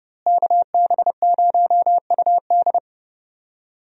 Morse code, heard as a keyed tone, about 30 wpm, 700 Hz. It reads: KB0UD
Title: USA Call Signs - 5 Characters - Repeated 3x - 30wpm